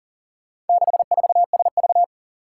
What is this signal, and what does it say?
Morse code, heard as a keyed tone, about 40 wpm, 700 Hz. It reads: 64SV
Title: Four Challenging Letter-Number Combinations 40wpm